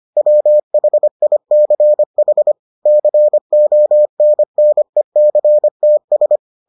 Morse code, heard as a keyed tone, about 25 wpm, 600 Hz. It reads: WHICH CONNECTS